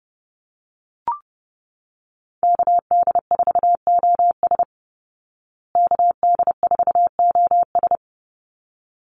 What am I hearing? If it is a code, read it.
Morse code, heard as a keyed tone, about 30 wpm, 700 Hz. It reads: KD4OS KD4OS
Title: Call Signs Repeated 3x - 30wpm